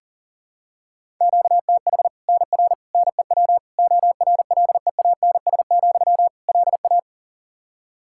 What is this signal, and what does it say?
Morse code, heard as a keyed tone, about 37 wpm, 700 Hz. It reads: QTH NR NEW ORLEANS, LA
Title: K1USN SST QSO Practice 40wpm